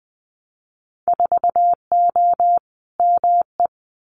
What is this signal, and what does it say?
Morse code, heard as a keyed tone, about 20 wpm, 700 Hz. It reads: 4O ME
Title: Winter Field Day Exchanges Encore 20wpm